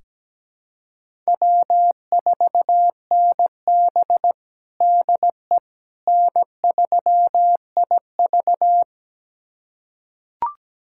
Morse code, heard as a keyed tone, about 17 wpm, 700 Hz. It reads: W4NB DE N3IV
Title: USA Callsign DE Callsign - 4 Characters - 17wpm